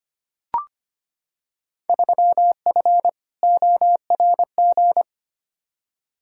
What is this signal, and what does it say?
Morse code, heard as a keyed tone, about 25 wpm, 700 Hz. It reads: 3F ORG
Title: Field Day Exchanges - 25wpm